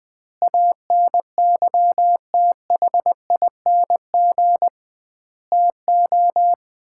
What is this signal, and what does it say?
Morse code, heard as a keyed tone, about 20 wpm, 700 Hz. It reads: ANYTHING TO
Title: 2x Word-Spacing - Sets of 5 Words - 20wpm